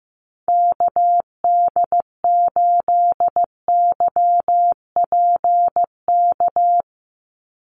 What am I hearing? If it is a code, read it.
Morse code, heard as a keyed tone, about 15 wpm, 700 Hz. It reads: KD8YPK
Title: USA Call Signs 15wpm